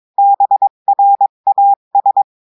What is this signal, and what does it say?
Morse code, heard as a keyed tone, about 22 wpm, 800 Hz. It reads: BRAS
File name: cours-du-mercredi / audios / cours-853-du-2-mai-2024-etxte chiffre et texte clair-a-20-wpm